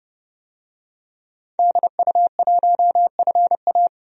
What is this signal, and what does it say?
Morse code, heard as a keyed tone, about 30 wpm, 700 Hz. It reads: DU1FA
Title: DXCC-100 Call Signs - 5 Characters - 30wpm